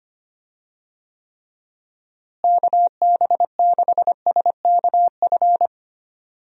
Morse code, heard as a keyed tone, about 25 wpm, 700 Hz. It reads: KB6SKF
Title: USA Call Signs - 6 Characters - Repeated 2x - 25wpm